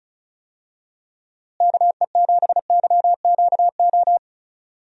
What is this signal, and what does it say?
Morse code, heard as a keyed tone, about 36 wpm, 700 Hz. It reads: KE7YQO
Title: USA Call Signs - 6 Characters - Repeated 2x - 35wpm